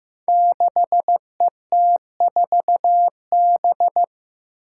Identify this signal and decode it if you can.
Morse code, heard as a keyed tone, about 15 wpm, 700 Hz. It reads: 6ET4B
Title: Five Challenging Letter-Number Combinations 15wpm